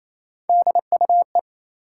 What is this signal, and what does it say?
Morse code, heard as a keyed tone, about 28 wpm, 700 Hz. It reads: DUE
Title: Three Letter Words - 28wpm